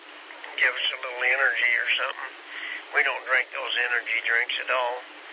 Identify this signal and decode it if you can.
Emergency Alert System (SAME) tones, heard as harsh buzzing bursts, with speech. It is heard from 1.2 to 4.5 s.